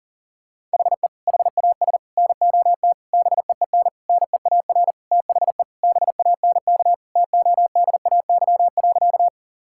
Morse code, heard as a keyed tone, about 40 wpm, 700 Hz. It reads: HE HAS NOT BEEN NEAR THE BANK TODAY.
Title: Sentences from Top 1000 Words - Review Part 6 - 40wpm